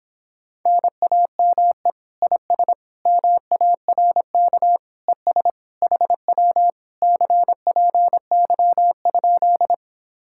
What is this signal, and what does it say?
Morse code, heard as a keyed tone, about 26 wpm, 700 Hz. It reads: NAME IS MARK ES HW CPY?